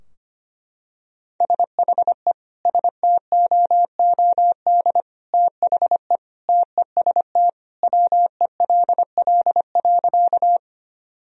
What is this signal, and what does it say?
Morse code, heard as a keyed tone, about 25 wpm, 700 Hz. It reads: SHE STOOD THE TEST WELL.
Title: Sentences from Top 1000 Words - Review Part 3 - 25wpm